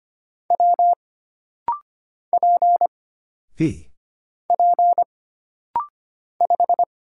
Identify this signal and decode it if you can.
Morse code, heard as a keyed tone, about 25 wpm, 700 Hz. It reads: W P P 5
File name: Single Character (Letter:Number:Punctuation:Prosign) - Rapid-Fire - 25wpm